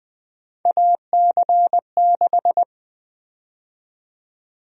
Morse code, heard as a keyed tone, about 20 wpm, 700 Hz. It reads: AC6
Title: USA Call Sign Prefixes - 20wpm